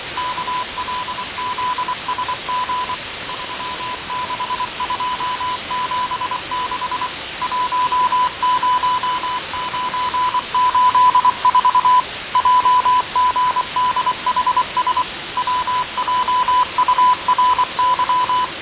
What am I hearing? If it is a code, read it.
Morse code, heard as a keyed tone, about 24 wpm, 1000 Hz, with radio noise. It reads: KLZSG 36276 10984 JGDHS WJURY
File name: fav22-6824usb-20251029-1420z-voice-test-win10